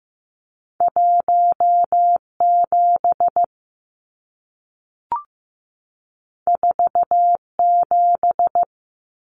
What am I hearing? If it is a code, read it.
Morse code, heard as a keyed tone, about 15 wpm, 700 Hz. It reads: 17 47
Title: Numbers - 2-Digit 15wpm